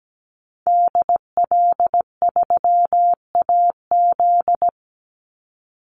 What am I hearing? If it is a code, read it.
Morse code, heard as a keyed tone, about 17 wpm, 700 Hz. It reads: DL3AZ
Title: Call Signs Repeated 3x - 17wpm